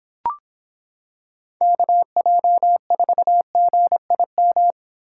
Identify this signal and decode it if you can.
Morse code, heard as a keyed tone, about 26 wpm, 700 Hz. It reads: KJ4GIM